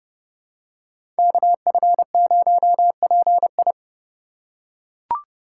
Morse code, heard as a keyed tone, about 31 wpm, 700 Hz. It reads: KF0PI
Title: USA Call Signs - 5 Characters - 2x Character Spacing - 30wpm